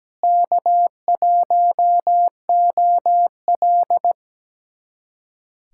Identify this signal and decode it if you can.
Morse code, heard as a keyed tone, about 17 wpm, 700 Hz. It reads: K1OL
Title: USA Callsign DE Callsign - 4 Characters - 17wpm